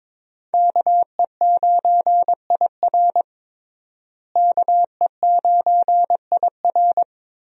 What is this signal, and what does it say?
Morse code, heard as a keyed tone, about 22 wpm, 700 Hz. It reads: KE9IR KE9IR
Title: USA Call Signs - 5 Characters - Repeated 2x - 22wpm